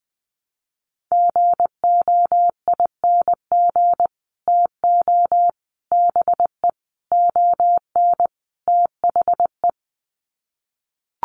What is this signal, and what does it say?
Morse code, heard as a keyed tone, about 20 wpm, 700 Hz. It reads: GOING TO BE ON THE
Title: Sets of 5 Words Encore2 20wpm